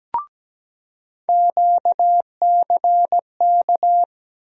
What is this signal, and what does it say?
Morse code, heard as a keyed tone, about 17 wpm, 700 Hz. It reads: QCK
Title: USA Call Sign Postfixes - 3 Letters 17wpm